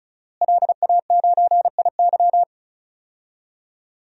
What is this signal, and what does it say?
Morse code, heard as a keyed tone, about 36 wpm, 700 Hz. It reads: LA9IY
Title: DXCC-100 Call Signs - 5 Characters - Repeated 2x - 35wpm